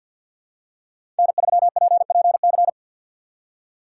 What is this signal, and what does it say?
Morse code, heard as a keyed tone, about 49 wpm, 700 Hz. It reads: N3PPC
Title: USA Call Signs - 5 Characters - 3x Character Spacing - 50wpm